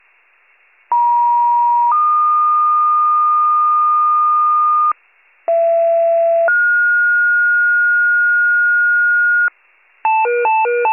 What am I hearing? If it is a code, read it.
Two-tone page, heard as a one-second tone, then a three-second tone, about 950 then 1220 Hz, about 670 then 1430 Hz, with noise. It is heard from 0.9 to 9.5 s.